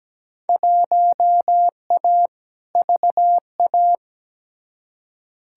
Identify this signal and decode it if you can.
Morse code, heard as a keyed tone, about 17 wpm, 700 Hz. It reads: 1A VA
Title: Field Day Exchanges - 17wpm